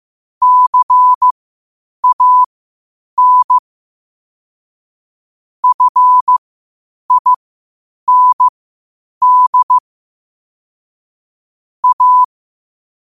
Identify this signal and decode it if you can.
Morse code, heard as a keyed tone, about 15 wpm, spaced as for 8 wpm, 1000 Hz. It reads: CAN FIND A
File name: qotdfiles / 8wpm